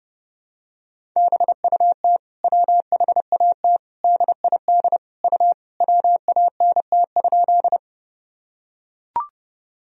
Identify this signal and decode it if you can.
Morse code, heard as a keyed tone, about 32 wpm, 700 Hz. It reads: BUT WHAT DID U WANT?
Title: Sentences from Top 200 Words and Abbreviations 30wpm